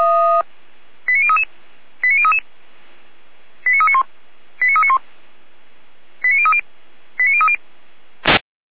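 Five-tone selective call (ZVEI), heard as quick runs of five tones, tones of about 80 ms. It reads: 89033 89033 89391 89391 89030 89030
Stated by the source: ZVEI five-tone call received by SDR.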